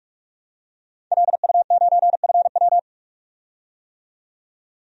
Morse code, heard as a keyed tone, about 41 wpm, 700 Hz. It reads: LU9FW
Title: DXCC-100 Call Signs - 5 Characters - 45wpm